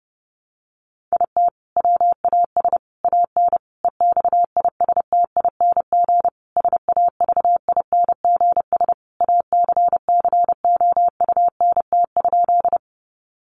Morse code, heard as a keyed tone, about 30 wpm, 700 Hz. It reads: IT WAS AN EXISTING SAVINGS ACCOUNT?
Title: Sentences from Top 1500 Words 30wpm